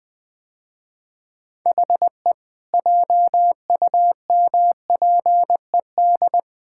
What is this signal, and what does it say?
Morse code, heard as a keyed tone, about 21 wpm, 700 Hz. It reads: HE JUMPED
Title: Sentences with 2-Word Phrases - Spoken Beginning - 20wpm